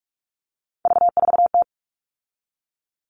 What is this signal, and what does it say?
Morse code, heard as a keyed tone, about 44 wpm, 700 Hz. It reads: V4T
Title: All Cut Numbers - 3-Digit 45wpm